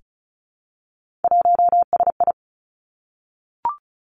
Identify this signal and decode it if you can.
Morse code, heard as a keyed tone, about 36 wpm, 700 Hz. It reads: 1SI
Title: Three Challenging Letter-Number Combinations 35wpm